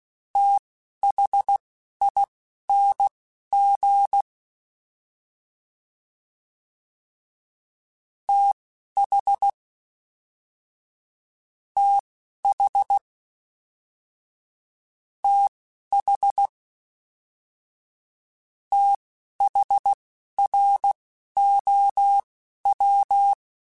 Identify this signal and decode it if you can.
Morse code, heard as a keyed tone, about 16 wpm, spaced as for 11 wpm, 800 Hz. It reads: THING TH TH TH THROW